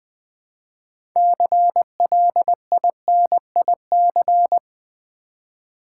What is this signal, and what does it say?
Morse code, heard as a keyed tone, about 20 wpm, 700 Hz. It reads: CLINIC
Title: 6 Letter Words - First 3 Letters Given - 20wpm